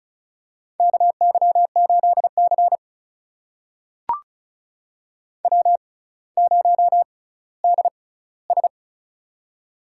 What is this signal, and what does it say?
Morse code, heard as a keyed tone, about 36 wpm, spaced as for 18 wpm, 700 Hz. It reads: KY8C W0DS
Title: USA Call Signs - 4 Characters - 6x Character Spacing - 35wpm